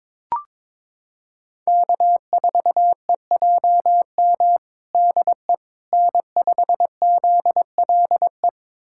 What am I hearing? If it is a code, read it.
Morse code, heard as a keyed tone, about 22 wpm, 700 Hz. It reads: K4EJM DE N5ZLE
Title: USA Callsign DE Callsign - 5 Characters - 22wpm